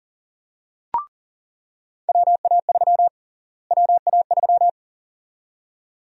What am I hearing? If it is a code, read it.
Morse code, heard as a keyed tone, about 40 wpm, 700 Hz. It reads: WA3 WA3
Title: USA Call Sign Prefixes - Repeated 2x - 40wpm